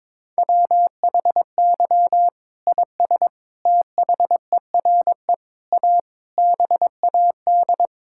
Morse code, heard as a keyed tone, about 22 wpm, 700 Hz. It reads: WHY IS THERE A BAD
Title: Sentences from Top 1600 Words 22wpm